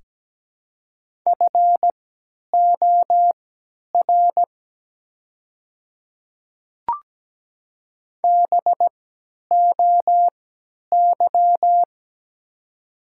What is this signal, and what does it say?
Morse code, heard as a keyed tone, about 17 wpm, spaced as for 10 wpm, 700 Hz. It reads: FOR BOY
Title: 3x Character Spacing - Top 500 Words - 17wpm